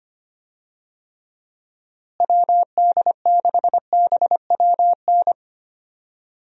Morse code, heard as a keyed tone, about 25 wpm, 700 Hz. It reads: WD6BWN
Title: USA Call Signs - 6 Characters - Repeated 2x - 25wpm